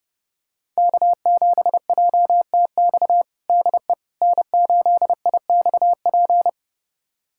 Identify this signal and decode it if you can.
Morse code, heard as a keyed tone, about 28 wpm, 700 Hz. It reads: K7JTX DE N8IXP
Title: USA Callsign DE Callsign - 5 Characters - 30wpm